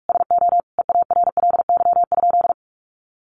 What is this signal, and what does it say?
Morse code, heard as a keyed tone, about 44 wpm, 700 Hz. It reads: SO EARLY?
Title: Sentences from Top 400 Words 45wpm